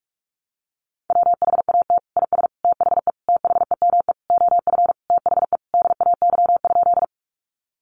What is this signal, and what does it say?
Morse code, heard as a keyed tone, about 44 wpm, 700 Hz. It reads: WHAT IS THE THEME OF THE DAY?
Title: Sentences from Top 1100 Words 45wpm